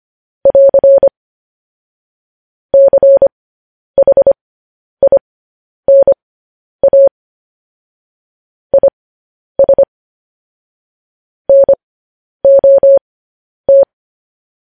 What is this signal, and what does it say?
Morse code, heard as a keyed tone, about 25 wpm, spaced as for 10 wpm, 550 Hz. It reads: <AR> CHINA IS NOT
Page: news headlines in morse code at 10 wpm